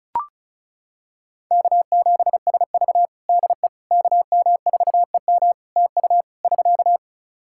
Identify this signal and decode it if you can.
Morse code, heard as a keyed tone, about 34 wpm, 700 Hz. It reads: K7SV DE KM4EM TU <SK>
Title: Practice QSO - Exchange 4 - Ending - 35wpm